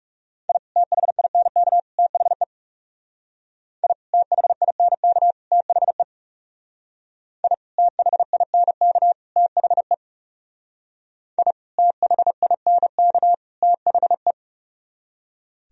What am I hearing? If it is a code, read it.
Morse code, heard as a keyed tone, about 35 wpm, 700 Hz. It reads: I THINK THE I THINK THE I THINK THE I THINK THE
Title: Speed-Racing - Sets of 3 Words 30wpm